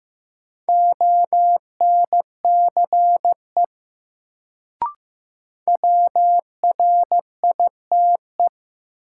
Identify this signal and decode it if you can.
Morse code, heard as a keyed tone, about 15 wpm, 700 Hz. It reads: ONCE WRITE
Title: Sentences from Top 300 Words - New Word Drill - 15wpm